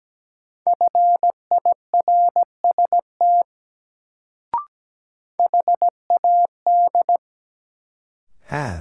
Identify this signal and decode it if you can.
Morse code, heard as a keyed tone, about 17 wpm, 700 Hz. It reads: FIRST HAD
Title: Sets of 2 Words - New Word Drill - 17wpm